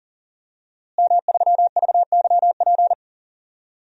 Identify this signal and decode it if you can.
Morse code, heard as a keyed tone, about 38 wpm, 700 Hz. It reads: M3VYP
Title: DXCC-100 Call Signs - 5 Characters - Repeated 3x - 40wpm